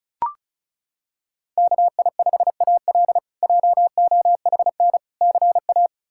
Morse code, heard as a keyed tone, about 34 wpm, 700 Hz. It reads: KI5AL JOHN CA
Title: K1USN SST QSO Practice - 35wpm